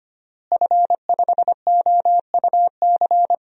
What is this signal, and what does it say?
Morse code, heard as a keyed tone, about 25 wpm, 700 Hz. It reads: F5OUC
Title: DXCC-100 Call Signs - 5 Characters - Repeated 2x - 25wpm